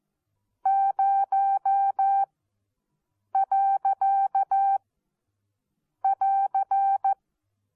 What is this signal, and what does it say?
Morse code, heard as a keyed tone, about 14 wpm, spaced as for 6 wpm, 780 Hz. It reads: 0.<AR>
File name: Onlinelehrgang / morsen / DJ4UF Morsen14